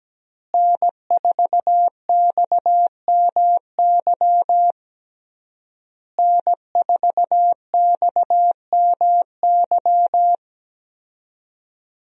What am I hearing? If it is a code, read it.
Morse code, heard as a keyed tone, about 17 wpm, 700 Hz. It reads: N4XMY N4XMY
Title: USA Call Signs - 5 Characters - Repeated 3x - 17wpm